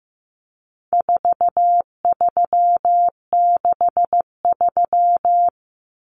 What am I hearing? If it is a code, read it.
Morse code, heard as a keyed tone, about 15 wpm, 700 Hz. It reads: 4363
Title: Common Cut Numbers - 4-Digit 15wpm